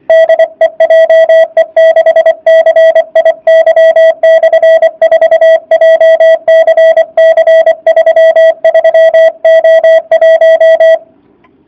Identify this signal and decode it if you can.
Morse code, heard as a keyed tone, about 24 wpm, 650 Hz. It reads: DE JE6CIY/4 JCC 33O1